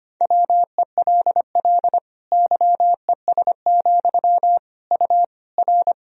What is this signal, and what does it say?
Morse code, heard as a keyed tone, about 25 wpm, 700 Hz. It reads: WELL YES, U R
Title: Sentences from Top 500 Words and Abbreviations 25wpm